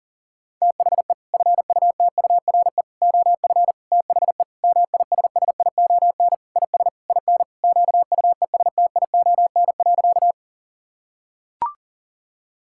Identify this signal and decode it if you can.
Morse code, heard as a keyed tone, about 37 wpm, 700 Hz. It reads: THE FUTURE OF THE MISSION IS IN QUESTION.
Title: Sentences from Top 1600 Words 40wpm